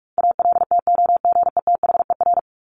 Morse code, heard as a keyed tone, about 44 wpm, 700 Hz. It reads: ALTOGETHER
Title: Top 1501-1750 Words 45wpm